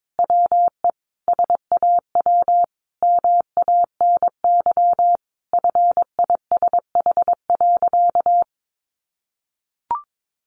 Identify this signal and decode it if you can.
Morse code, heard as a keyed tone, about 22 wpm, 700 Hz. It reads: WE SAW MANY FISH.
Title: Sentences from Top 1500 Words 22wpm